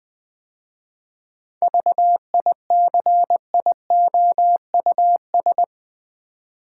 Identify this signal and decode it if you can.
Morse code, heard as a keyed tone, about 20 wpm, 700 Hz. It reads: VICIOUS
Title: 7 Letter Words - First 3 Letters Given - 20wpm